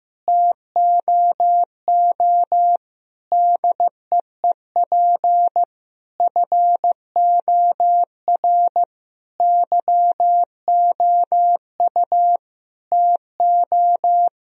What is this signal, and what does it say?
Morse code, heard as a keyed tone, about 15 wpm, 700 Hz. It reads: TOO DEEP FOR YOU TO
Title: Sentences from Top 1000 Words - Review Part 3 - 15wpm